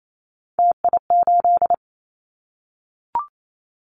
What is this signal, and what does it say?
Morse code, heard as a keyed tone, about 28 wpm, 700 Hz. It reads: TI8
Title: Callsign Prefixes for Top 100 DXCC - 28wpm